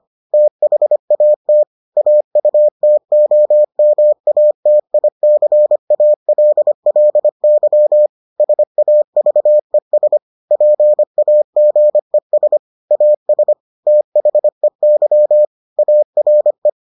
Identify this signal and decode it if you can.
Morse code, heard as a keyed tone, about 25 wpm, 600 Hz. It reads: THAT AUTOMATICALLY SAVES PAGES AS THEY ARE